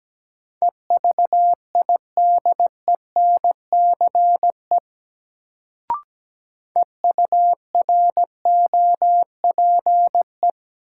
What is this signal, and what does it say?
Morse code, heard as a keyed tone, about 17 wpm, 700 Hz. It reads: EVIDENCE EUROPE
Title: Sets of 5 Words Encore - New Word Drill - 17wpm